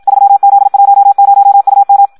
Morse code, heard as a keyed tone, about 54 wpm, 800 Hz. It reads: 3Z10UM
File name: Audio practica CW / qrq-tx-54 2